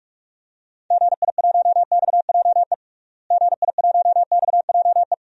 Morse code, heard as a keyed tone, about 41 wpm, 700 Hz. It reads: GI1XJE GI1XJE
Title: DXCC-100 Call Signs - 6 Characters - Repeated 2x - 45wpm